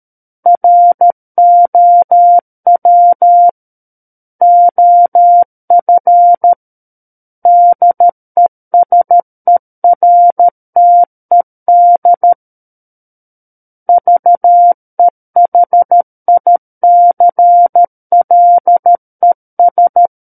Morse code, heard as a keyed tone, about 13 wpm, 700 Hz. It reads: ROW OF DESERTED VEHICLES
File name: Book One / 04-Chapter 4 - 13 WPM